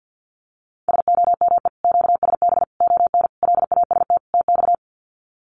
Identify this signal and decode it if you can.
Morse code, heard as a keyed tone, about 50 wpm, 700 Hz. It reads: SOME QSB ON LAST TX
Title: Common QSO Phrases 50wpm